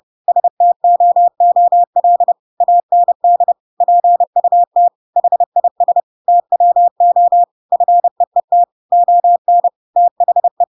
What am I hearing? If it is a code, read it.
Morse code, heard as a keyed tone, about 30 wpm, 700 Hz. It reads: STOOL AND PUT HIS TWO FEET ON THE
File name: Pinocchio / Pinocchio ch6 30 30 ik1hge